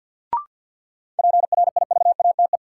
Morse code, heard as a keyed tone, about 51 wpm, 700 Hz. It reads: PRIVATE
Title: Sentences from Top 1000 Words - New Word Drill - 50wpm